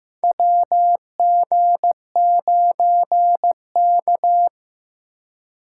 Morse code, heard as a keyed tone, about 15 wpm, 700 Hz. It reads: WG9K
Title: USA Call Signs - 4 Characters - 6x Character Spacing - 15wpm